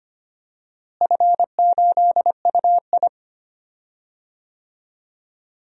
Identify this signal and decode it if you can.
Morse code, heard as a keyed tone, about 25 wpm, 700 Hz. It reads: F8UI
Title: DXCC-100 Call Signs - 4 Characters - Repeated 2x - 25wpm